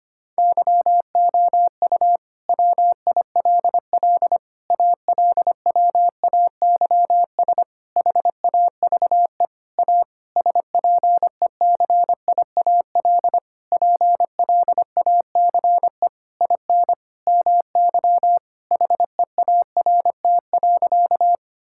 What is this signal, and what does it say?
Morse code, heard as a keyed tone, about 25 wpm, 700 Hz. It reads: YOU WILL ALWAYS HAVE A SPECIAL PLACE IN MY HEART.
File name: practice / 25wpm / Sentences with 9-Word Phrases - 25wpm